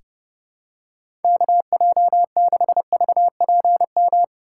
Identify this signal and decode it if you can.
Morse code, heard as a keyed tone, about 32 wpm, 700 Hz. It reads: KJ6VPM
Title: USA Call Signs - 2x Character Spacing - 30wpm